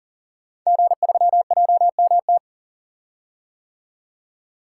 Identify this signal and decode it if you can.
Morse code, heard as a keyed tone, about 39 wpm, 700 Hz. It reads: G3JMT
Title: Code Groups - 5 Letter:Numbers - 40wpm